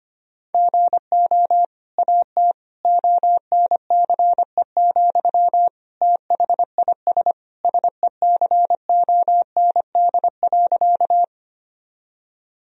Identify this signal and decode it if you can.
Morse code, heard as a keyed tone, about 25 wpm, 700 Hz. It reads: GO AT ONCE, THIS SECOND.
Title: Sentences from Top 1000 Words - Review Part 1 - 25wpm